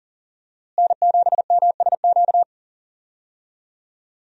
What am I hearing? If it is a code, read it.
Morse code, heard as a keyed tone, about 37 wpm, 700 Hz. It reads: N7MSQ
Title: USA Call Signs - 5 Characters - Repeated 3x - 40wpm